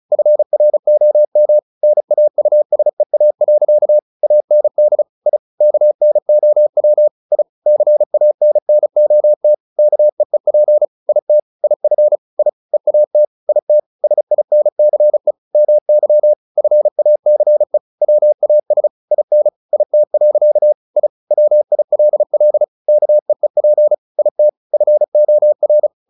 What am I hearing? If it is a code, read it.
Morse code, heard as a keyed tone, about 34 wpm, 600 Hz. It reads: FROM NAUSEA. AND I KNOW I CANNOT KEEP IT IF I EAT IT SINCE MY FACE WAS IN IT. I WILL KEEP IT FOR
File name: the old man and the sea 2 35wpm